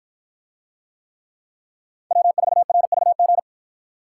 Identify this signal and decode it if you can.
Morse code, heard as a keyed tone, about 53 wpm, 700 Hz. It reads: W4RVG